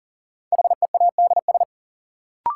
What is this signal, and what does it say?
Morse code, heard as a keyed tone, about 40 wpm, 700 Hz. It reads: HEADS
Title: Sentences from Top 1000 Words - Review Part 3 - New Word Drill - 40wpm